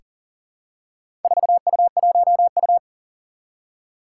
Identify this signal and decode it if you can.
Morse code, heard as a keyed tone, about 39 wpm, 700 Hz. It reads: 4U1U
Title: Four Challenging Letter-Number Combinations 40wpm